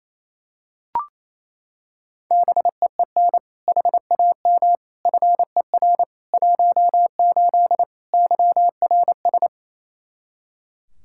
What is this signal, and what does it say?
Morse code, heard as a keyed tone, about 28 wpm, 700 Hz. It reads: BEEN HAM FER 18 YRS
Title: Practice QSO - Exchange 3 - More Info - 28wpm